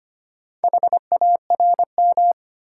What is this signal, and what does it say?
Morse code, heard as a keyed tone, about 25 wpm, 700 Hz. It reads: HARM
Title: Four Letter Words - 25wpm